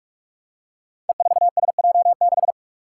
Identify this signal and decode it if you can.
Morse code, heard as a keyed tone, about 46 wpm, 700 Hz. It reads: E4SJ6
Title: Five Challenging Letter-Number Combinations 45wpm